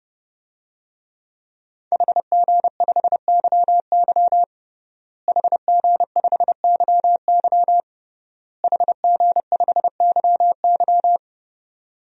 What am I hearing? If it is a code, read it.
Morse code, heard as a keyed tone, about 29 wpm, 700 Hz. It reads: HG5YY HG5YY HG5YY
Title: DXCC-100 Call Signs - 5 Characters - Repeated 3x - 30wpm